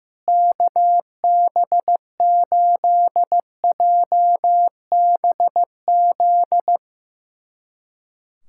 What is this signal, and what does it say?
Morse code, heard as a keyed tone, about 16 wpm, 700 Hz. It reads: KB8JBZ
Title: USA Callsign DE Callsign 15wpm